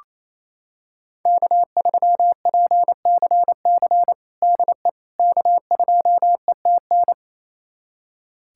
Morse code, heard as a keyed tone, about 28 wpm, 700 Hz. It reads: K3PCC DE K2ETN
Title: USA Callsign DE Callsign - 5 Characters - 28wpm